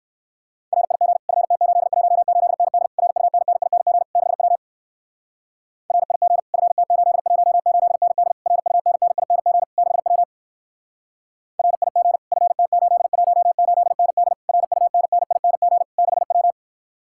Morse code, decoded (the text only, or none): RIG FT818ND RUNNING 6W RIG FT818ND RUNNING 6W RIG FT818ND RUNNING 6W